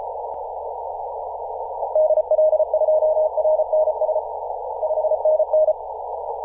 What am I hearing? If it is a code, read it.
Morse code, heard as a keyed tone, about 34 wpm, 620 Hz, with noise. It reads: DL2RNS 5NN
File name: posts / 2017-aug-07-0652z-ri0li-14-036mhz-cw